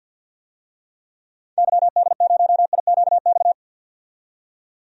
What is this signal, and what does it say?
Morse code, heard as a keyed tone, about 52 wpm, 700 Hz. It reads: YD0IQX
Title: DXCC-100 Call Signs - 6 Characters - 50wpm